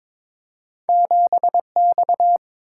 Morse code, heard as a keyed tone, about 22 wpm, 700 Hz. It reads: 7X
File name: practice / 22wpm / Two Letter-Number Combinations - 22wpm